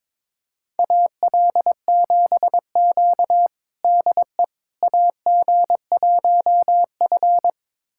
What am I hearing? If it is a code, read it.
Morse code, heard as a keyed tone, about 22 wpm, 700 Hz. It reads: AL7Q DE AG1F
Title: USA Callsign DE Callsign - 4 Characters - 22wpm